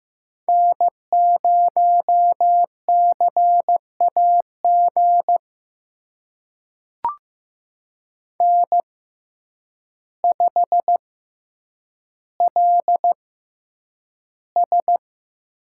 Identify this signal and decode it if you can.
Morse code, heard as a keyed tone, about 15 wpm, spaced as for 8 wpm, 700 Hz. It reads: N0CAG N5LS
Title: USA Call Signs - 5 Characters - 6x Character Spacing - 15wpm